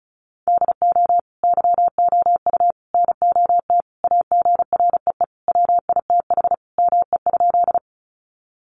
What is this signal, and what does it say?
Morse code, heard as a keyed tone, about 34 wpm, 700 Hz. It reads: DO YOU NOT AGREE WITH ME?
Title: Sentences from Top 1100 Words 35wpm